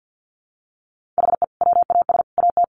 Morse code, heard as a keyed tone, about 50 wpm, 700 Hz. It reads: HE WAS AT
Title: Sentences from Top 200 Words 50wpm